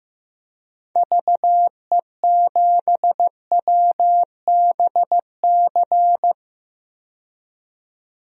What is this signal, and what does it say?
Morse code, heard as a keyed tone, about 15 wpm, 700 Hz. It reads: VE7WBC
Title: DXCC-100 Call Signs - 6 Characters - 15wpm